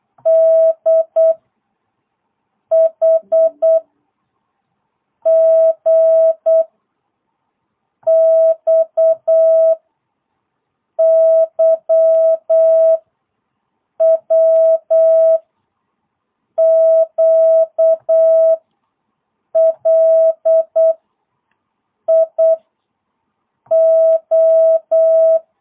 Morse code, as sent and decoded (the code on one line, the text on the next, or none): -.. / .... / --. / -..- / -.-- / .-- / --.- / .-.. / .. / ---
D H G X Y W Q L I O